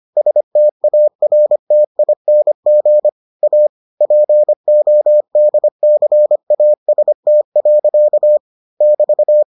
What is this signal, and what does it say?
Morse code, heard as a keyed tone, about 25 wpm, 600 Hz. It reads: STARTING A PODCAST. <BT>